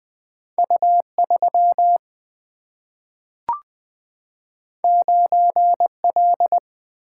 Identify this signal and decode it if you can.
Morse code, heard as a keyed tone, about 20 wpm, 700 Hz. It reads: U3 9L
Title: Single-Letter Single-Number 20wpm